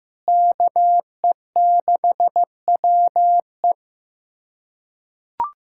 Morse code, heard as a keyed tone, about 16 wpm, 700 Hz. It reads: KE6WE
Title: USA Callsign DE Callsign - 5 Characters - 15wpm